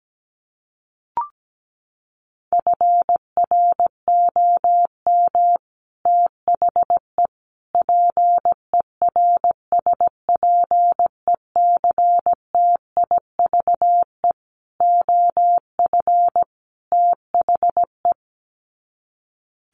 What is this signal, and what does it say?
Morse code, heard as a keyed tone, about 17 wpm, 700 Hz. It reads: FROM THE PERSPECTIVE OF THE
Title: Sets of 5 Words Encore2 17wpm